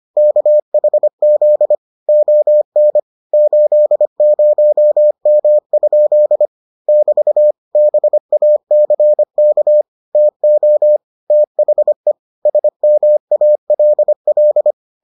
Morse code, heard as a keyed tone, about 25 wpm, 600 Hz. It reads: KHZ ON 80M? <BT> BACK TO THE SMALL